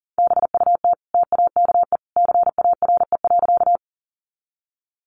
Morse code, heard as a keyed tone, about 40 wpm, 700 Hz. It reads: BUT TAKE CARE.
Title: Sentences from Top 300 Words 40wpm